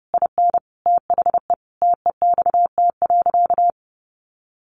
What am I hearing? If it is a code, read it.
Morse code, heard as a keyed tone, about 30 wpm, 700 Hz. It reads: IN THE TEXT.
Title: Sentences from Top 1200 Words 30wpm